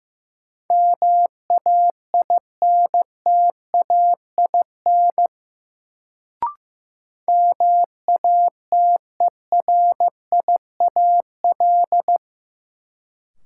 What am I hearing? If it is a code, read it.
Morse code, heard as a keyed tone, about 16 wpm, 700 Hz. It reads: MAINTAIN MATERIAL
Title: Sentences from Top 1000 Words - Review Part 5 - New Word Drill - 15wpm